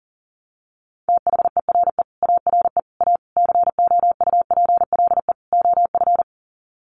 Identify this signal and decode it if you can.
Morse code, heard as a keyed tone, about 40 wpm, 700 Hz. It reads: THERE ARE A COUPLE OF
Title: Sets of 5 Words Encore 40wpm